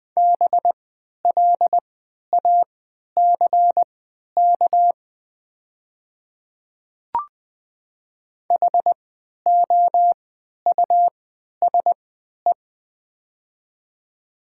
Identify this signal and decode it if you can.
Morse code, heard as a keyed tone, about 20 wpm, spaced as for 11 wpm, 700 Hz. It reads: BLACK HOUSE
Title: 3x Character Spacing - Top 500 Words - 20wpm